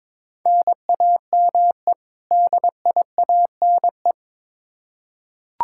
Morse code, heard as a keyed tone, about 22 wpm, 700 Hz. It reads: NAME DIANE
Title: K1USN SST QSO Practice 22wpm